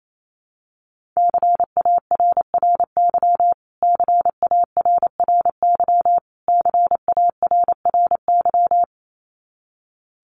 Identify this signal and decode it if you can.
Morse code, heard as a keyed tone, about 28 wpm, 700 Hz. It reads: CARRY CARRY CARRY
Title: Top 500 Words Repeated 28wpm